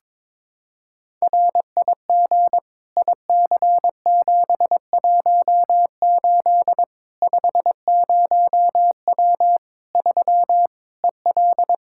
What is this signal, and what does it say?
Morse code, heard as a keyed tone, about 22 wpm, 700 Hz. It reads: RIG IC718 50W 3 EL
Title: Practice QSO - Exchange 2 - Info - 22wpm